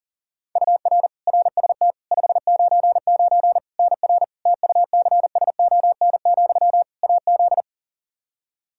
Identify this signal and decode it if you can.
Morse code, heard as a keyed tone, about 37 wpm, 700 Hz. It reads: UR RST 599 NR TUCSON, AZ
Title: Practice QSO - Exchange 1 - Beginning - 40wpm